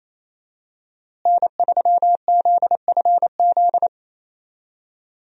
Morse code, heard as a keyed tone, about 28 wpm, 700 Hz. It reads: N3ZFZ
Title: USA Call Signs - 5 Characters - Repeated 2x - 28wpm